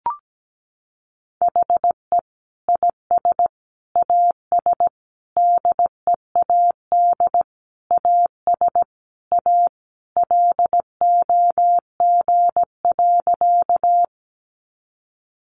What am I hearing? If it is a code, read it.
Morse code, heard as a keyed tone, about 17 wpm, 700 Hz. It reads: HE IS AS DEAD AS A LOG.
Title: Sentences from Top 1500 Words 17wpm